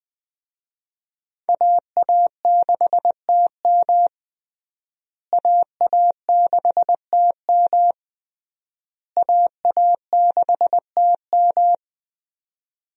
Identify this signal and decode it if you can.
Morse code, heard as a keyed tone, about 21 wpm, 700 Hz. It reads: AA6TM AA6TM AA6TM
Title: USA Call Signs - 5 Characters - Repeated 3x - 20wpm